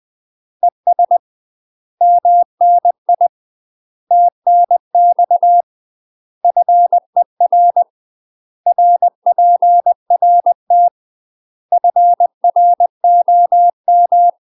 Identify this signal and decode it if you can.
Morse code, heard as a keyed tone, about 20 wpm, 700 Hz. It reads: ES MNI TNX FER RPRT FROM
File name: UK MASTER QSOs TRAINING / 20WPM / UK MASTER QSOs 0063